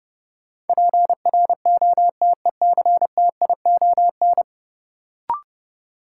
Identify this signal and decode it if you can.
Morse code, heard as a keyed tone, about 32 wpm, 700 Hz. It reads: PROTECTION
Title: Sentences from Top 1600 Words - New Word Drill Alt - 30wpm